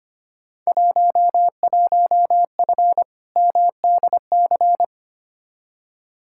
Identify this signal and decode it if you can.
Morse code, heard as a keyed tone, about 25 wpm, 700 Hz. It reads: 11F MDC
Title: Field Day Exchanges Encore - 25wpm